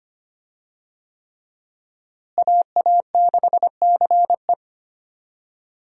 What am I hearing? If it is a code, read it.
Morse code, heard as a keyed tone, about 25 wpm, 700 Hz. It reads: AA6CE
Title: USA Call Signs - 5 Characters - Repeated 3x - 25wpm